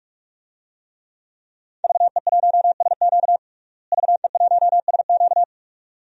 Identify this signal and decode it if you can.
Morse code, heard as a keyed tone, about 42 wpm, 700 Hz. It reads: VE1SQ VE1SQ
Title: DXCC-100 Call Signs - 5 Characters - Repeated 2x - 45wpm